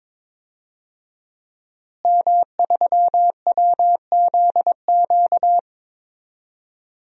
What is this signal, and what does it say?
Morse code, heard as a keyed tone, about 22 wpm, 700 Hz. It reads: M3WZQ
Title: DXCC-100 Call Signs - 5 Characters - Repeated 2x - 22wpm